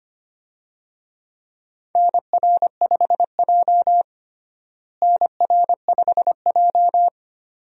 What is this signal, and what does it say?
Morse code, heard as a keyed tone, about 25 wpm, 700 Hz. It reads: NR5J NR5J
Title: USA Call Signs - 4 Characters - Repeated 2x - 25wpm